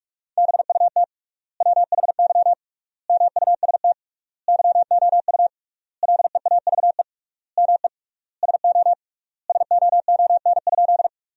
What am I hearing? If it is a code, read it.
Morse code, heard as a keyed tone, about 43 wpm, 700 Hz. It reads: BUT WHY MUST YOU LEAVE ME SO SOON?
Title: 3x Word-Spacing - Sentences from Top 400 Words - 45wpm